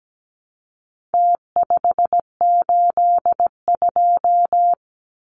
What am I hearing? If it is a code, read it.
Morse code, heard as a keyed tone, about 17 wpm, 700 Hz. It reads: T582
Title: Numbers - Contest Style - 4-Digit 17wpm